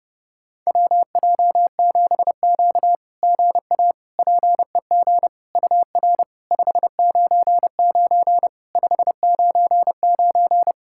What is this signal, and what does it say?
Morse code, heard as a keyed tone, about 30 wpm, 690 Hz. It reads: WJ7Q GA PEG UR 599 599
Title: SOTA QSO Practice 30wpm